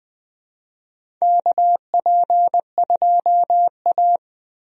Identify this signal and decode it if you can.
Morse code, heard as a keyed tone, about 20 wpm, 700 Hz. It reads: KP2A
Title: DXCC-100 Call Signs - 4 Characters - Repeated 3x - 20wpm